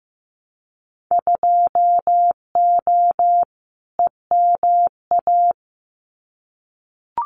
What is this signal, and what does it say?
Morse code, heard as a keyed tone, about 15 wpm, 700 Hz. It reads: 2O EMA
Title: Winter Field Day Exchanges 15wpm